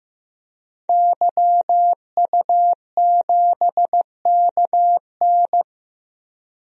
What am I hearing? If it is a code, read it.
Morse code, heard as a keyed tone, about 16 wpm, 700 Hz. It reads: YU7KN
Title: DXCC-100 Call Signs - 5 Characters - 15wpm